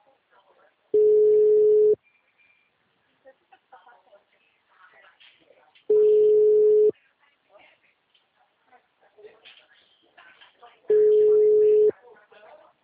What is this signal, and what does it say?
Ringback tone (Europe, 425 Hz), with speech.
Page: TLMK - Rio de janeiro 2